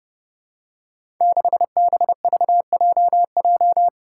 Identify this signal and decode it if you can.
Morse code, heard as a keyed tone, about 30 wpm, 700 Hz. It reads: 6BVJJ
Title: Five Challenging Letter-Number Combinations 30wpm